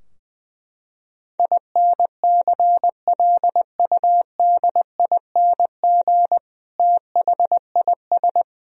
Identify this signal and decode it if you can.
Morse code, heard as a keyed tone, about 20 wpm, 700 Hz. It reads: INCLUDING THIS
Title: Sentences from Top 1000 Words - Review Part 4 - 20wpm